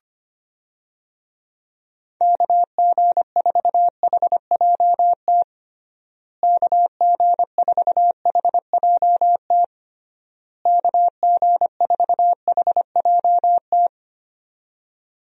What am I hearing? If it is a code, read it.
Morse code, heard as a keyed tone, about 25 wpm, 700 Hz. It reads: KG4HJT KG4HJT KG4HJT
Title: USA Call Signs - 6 Characters - Repeated 3x - 25wpm